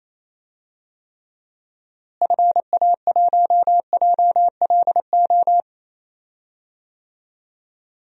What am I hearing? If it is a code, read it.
Morse code, heard as a keyed tone, about 28 wpm, 700 Hz. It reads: FA1JLO
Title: DXCC-100 Call Signs - 6 Characters - Repeated 2x - 28wpm